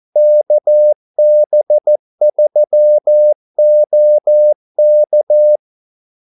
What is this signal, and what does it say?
Morse code, heard as a keyed tone, about 14 wpm, 600 Hz. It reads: KB3OK
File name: Callsigns-14wpm